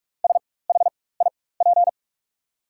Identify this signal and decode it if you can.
Morse code, heard as a keyed tone, about 43 wpm, spaced as for 21 wpm, 690 Hz. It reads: SHIP
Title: 4x Character Spacing - Top 500 Words - 45wpm